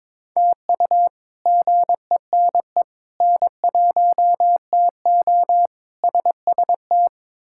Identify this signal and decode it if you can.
Morse code, heard as a keyed tone, about 22 wpm, 700 Hz. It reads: TU GENE N1TO SST
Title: K1USN SST QSO Practice - 22wpm